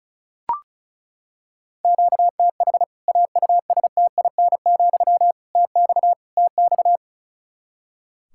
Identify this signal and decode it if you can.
Morse code, heard as a keyed tone, about 36 wpm, 700 Hz. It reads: QTH AUSTIN, TX TX
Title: K1USN SST QSO Practice 35wpm